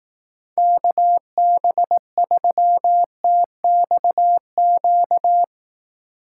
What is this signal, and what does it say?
Morse code, heard as a keyed tone, about 18 wpm, 700 Hz. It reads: KB3TXQ
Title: Speed Racing - USA 2x3 Call Signs 15wpm